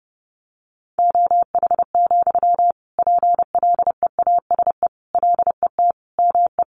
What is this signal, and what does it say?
Morse code, heard as a keyed tone, about 30 wpm, 700 Hz. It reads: OH, PLEASE LET ME
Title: Sentences from Top 1500 Words 30wpm